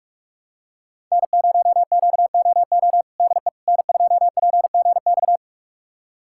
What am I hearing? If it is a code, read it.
Morse code, heard as a keyed tone, about 44 wpm, 700 Hz. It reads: N0QOO DE N2PGX
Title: USA Callsign DE Callsign - 5 Characters - 45wpm